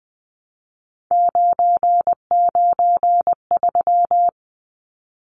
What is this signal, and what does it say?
Morse code, heard as a keyed tone, about 20 wpm, 700 Hz. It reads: 993
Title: Numbers - 3-Digit 20wpm